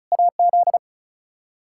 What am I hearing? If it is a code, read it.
Morse code, heard as a keyed tone, about 34 wpm, 700 Hz. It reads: AZ
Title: Practice QSO - Exchange 1 - Beginning - 35wpm